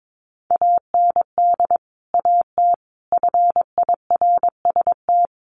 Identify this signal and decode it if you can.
Morse code, heard as a keyed tone, about 22 wpm, 700 Hz. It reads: AND AT FIRST
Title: Sentences from Top 1300 Words 22wpm